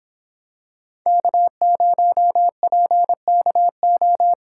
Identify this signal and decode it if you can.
Morse code, heard as a keyed tone, about 25 wpm, 700 Hz. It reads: K0PKO